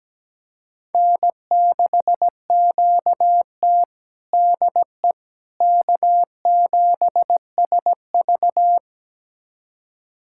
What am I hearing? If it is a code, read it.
Morse code, heard as a keyed tone, about 17 wpm, 700 Hz. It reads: N6QT DE K7SV
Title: USA Callsign DE Callsign - 4 Characters - 17wpm